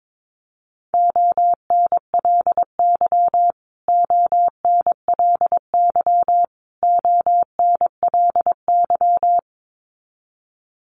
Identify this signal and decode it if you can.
Morse code, heard as a keyed tone, about 22 wpm, 700 Hz. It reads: ONLY ONLY ONLY
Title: Top 500 Words Repeated 22wpm